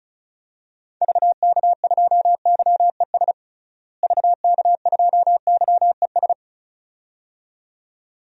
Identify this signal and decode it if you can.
Morse code, heard as a keyed tone, about 36 wpm, 700 Hz. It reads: VK2YES VK2YES
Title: DXCC-100 Call Signs - 6 Characters - Repeated 2x - 35wpm